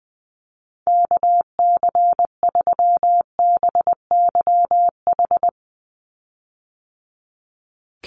Morse code, heard as a keyed tone, about 20 wpm, 700 Hz. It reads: KC3BYH
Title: USA Call Signs 20wpm